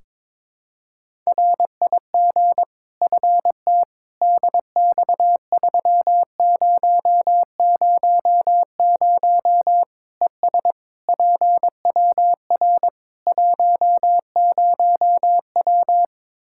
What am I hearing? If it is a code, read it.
Morse code, heard as a keyed tone, about 22 wpm, 700 Hz. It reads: RIG FT DX3000 ES PWR 10W
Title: Practice QSO - Exchange 2 - Info - 22wpm